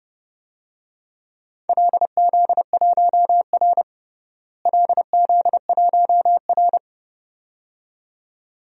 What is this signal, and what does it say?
Morse code, heard as a keyed tone, about 32 wpm, 700 Hz. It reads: LZ1R LZ1R
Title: DXCC-100 Call Signs - 4 Characters - Repeated 2x - 30wpm